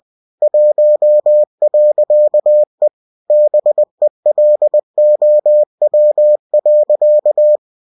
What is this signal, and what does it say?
Morse code, heard as a keyed tone, about 20 wpm, 600 Hz. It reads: 1.E BELOW.
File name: cw-books / winnie-the-pooh / winnie-20wpm-page-0113